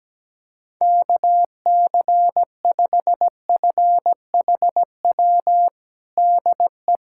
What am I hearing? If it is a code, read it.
Morse code, heard as a keyed tone, about 17 wpm, 700 Hz. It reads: KC5FHW DE
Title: USA Callsign DE Callsign 17wpm